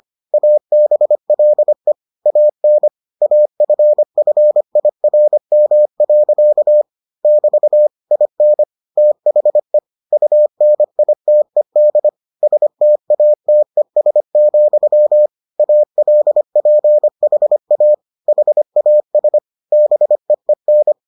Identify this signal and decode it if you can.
Morse code, heard as a keyed tone, about 25 wpm, 600 Hz. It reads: ABLE AN AFFIRM. <BT> IN THE UNITED STATES, ALPHA HAS BEEN